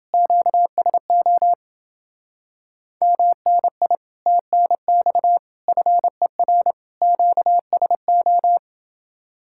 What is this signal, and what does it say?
Morse code, heard as a keyed tone, about 29 wpm, 700 Hz. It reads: QSO MNI TNX FER QSO